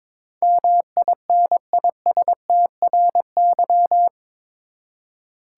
Morse code, heard as a keyed tone, about 22 wpm, 700 Hz. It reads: MINISTRY
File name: practice / 22wpm / Eight Letter Words - 22wpm